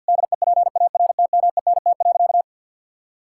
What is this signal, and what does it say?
Morse code, heard as a keyed tone, about 52 wpm, 700 Hz. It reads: DEPARTMENT.
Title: Sentences from Top 1000 Words - Review Part 5 - 50wpm